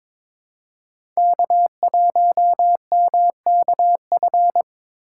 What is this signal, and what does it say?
Morse code, heard as a keyed tone, about 22 wpm, 700 Hz. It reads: K1MKF
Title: USA Call Signs - 5 Characters - 3x Character Spacing - 22wpm